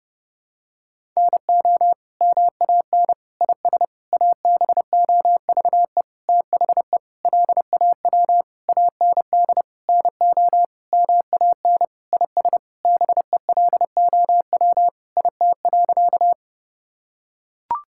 Morse code, heard as a keyed tone, about 30 wpm, 700 Hz. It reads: NO MAN IS ABOVE THE LAW AND NO MAN IS BELOW IT.
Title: Sentences from Top 1000 Words - Review Part 5 - 30wpm